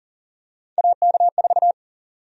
Morse code, heard as a keyed tone, about 39 wpm, 700 Hz. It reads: AK4
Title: USA Call Sign Prefixes - Repeated 2x - 40wpm